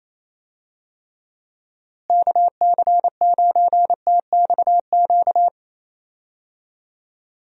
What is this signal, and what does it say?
Morse code, heard as a keyed tone, about 28 wpm, 700 Hz. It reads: KC9TXQ
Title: USA Call Signs - 6 Characters - Repeated 2x - 28wpm